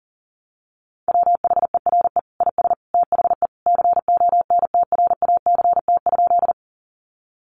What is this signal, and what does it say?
Morse code, heard as a keyed tone, about 40 wpm, 700 Hz. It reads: WHERE IS THE CONTRACT?
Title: Sentences from Top 1400 Words 40wpm